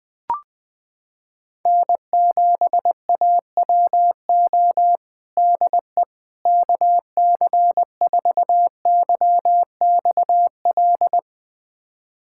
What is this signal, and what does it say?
Morse code, heard as a keyed tone, about 21 wpm, 700 Hz. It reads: N7AWO DE KC4YXL
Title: USA Callsign DE Callsign 20wpm